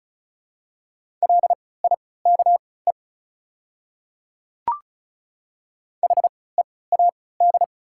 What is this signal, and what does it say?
Morse code, heard as a keyed tone, about 34 wpm, spaced as for 20 wpm, 690 Hz. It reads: LIKE HEAD
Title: 3x Character Spacing - Top 300 Words - 35wpm